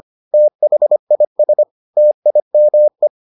Morse code, heard as a keyed tone, about 25 wpm, 600 Hz. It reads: THIS TIME